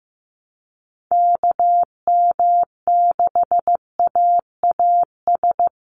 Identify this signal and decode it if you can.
Morse code, heard as a keyed tone, about 15 wpm, 700 Hz. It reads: KM6AAS
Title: USA Call Signs 15wpm